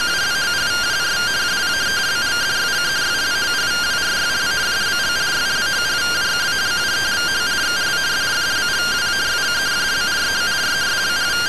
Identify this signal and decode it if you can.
Two-tone teleprinter signal, about 170 Hz shift.
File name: findings / MR AXIAL 0mm 6mm.gen-17 rtty rx